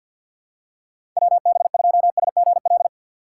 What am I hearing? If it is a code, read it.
Morse code, heard as a keyed tone, about 48 wpm, 700 Hz. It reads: WB2SGL
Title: USA Call Signs - 6 Characters - 6x Character Spacing - 50wpm